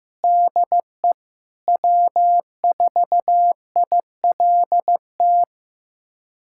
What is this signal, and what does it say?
Morse code, heard as a keyed tone, about 15 wpm, 700 Hz. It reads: DE W4ILT
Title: USA Callsign DE Callsign - 5 Characters - 15wpm